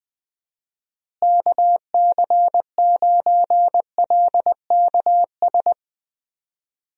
Morse code, heard as a keyed tone, about 21 wpm, 700 Hz. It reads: KC9LKS
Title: USA Call Signs - 6 Characters - Repeated 3x - 20wpm